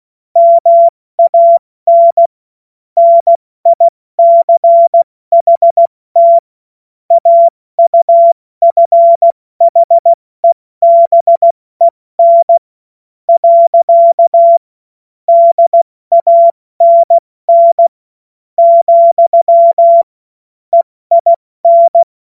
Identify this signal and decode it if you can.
Morse code, heard as a keyed tone, about 16 wpm, spaced as for 13 wpm, 680 Hz. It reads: MAN NICHT AUFHEBEN. DANN, EIN
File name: Morsekurs / Witze / Witz 17. August 2021, 70 F80